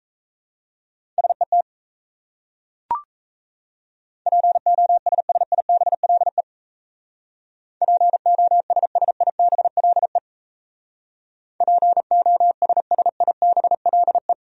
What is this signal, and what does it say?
Morse code, heard as a keyed tone, about 37 wpm, 700 Hz. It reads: SET POSSIBLE POSSIBLE POSSIBLE